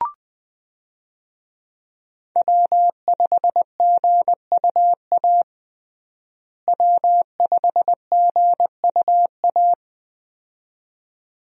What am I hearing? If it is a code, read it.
Morse code, heard as a keyed tone, about 20 wpm, 700 Hz. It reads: W5GUA W5GUA
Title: USA Call Signs - 5 Characters - Repeated 2x - 20wpm